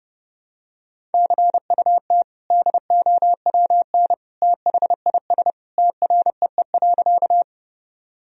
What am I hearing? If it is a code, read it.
Morse code, heard as a keyed tone, about 30 wpm, 700 Hz. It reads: CUT DOWN THIS TREE.
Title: Sentences from Top 1000 Words - Review Part 1 - 30wpm